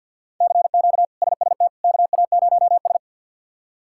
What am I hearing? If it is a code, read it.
Morse code, heard as a keyed tone, about 53 wpm, 700 Hz. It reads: CQ SST KA0S
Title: K1USN SST QSO Practice - 50wpm